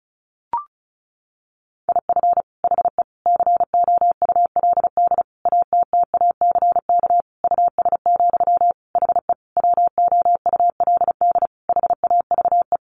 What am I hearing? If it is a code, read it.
Morse code, heard as a keyed tone, about 34 wpm, 700 Hz. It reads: IF HE COULD ATTACK US, HE WOULD HAVE
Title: Sentences from Top 1500 Words 35wpm